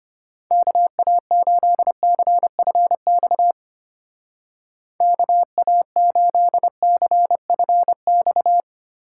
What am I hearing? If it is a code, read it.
Morse code, heard as a keyed tone, about 32 wpm, 700 Hz. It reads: KA8CFX KA8CFX